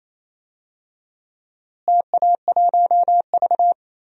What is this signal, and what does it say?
Morse code, heard as a keyed tone, about 28 wpm, 700 Hz. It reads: TA1V
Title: DXCC-100 Call Signs - 4 Characters - Repeated 3x - 28wpm